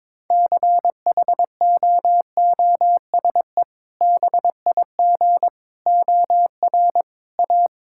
Morse code, heard as a keyed tone, about 22 wpm, 700 Hz. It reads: CHOOSE BIG OR A
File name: practice / 22wpm / Sentences with 2-Word Phrases - 22wpm